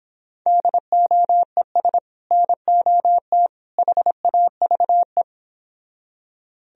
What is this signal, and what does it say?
Morse code, heard as a keyed tone, about 29 wpm, 700 Hz. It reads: DOES NOT HAVE